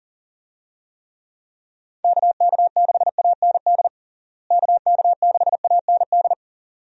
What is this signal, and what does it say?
Morse code, heard as a keyed tone, about 40 wpm, 700 Hz. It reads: KK6AND KK6AND
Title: USA Call Signs - 6 Characters - Repeated 2x - 40wpm